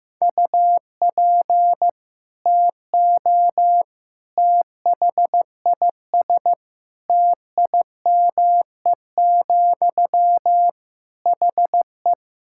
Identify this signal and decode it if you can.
Morse code, heard as a keyed tone, about 16 wpm, 700 Hz. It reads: UP TO THIS TIME, HE
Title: Sentences from Top 1000 Words - Review Part 6 - 15wpm